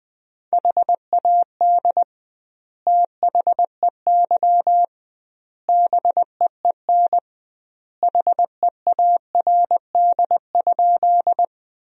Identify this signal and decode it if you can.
Morse code, heard as a keyed tone, about 20 wpm, 700 Hz. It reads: HAD THEY BEEN HEARD?
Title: 2x Word-Spacing - Sentences from Top 400 Words - 20wpm